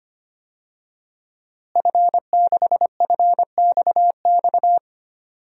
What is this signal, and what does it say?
Morse code, heard as a keyed tone, about 25 wpm, 700 Hz. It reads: F6FXX
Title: DXCC-100 Call Signs - 5 Characters - Repeated 3x - 25wpm